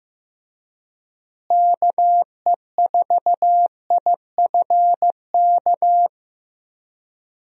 Morse code, heard as a keyed tone, about 15 wpm, 700 Hz. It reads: KE4IFK
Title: USA Call Signs - 6 Characters - Repeated 3x - 15wpm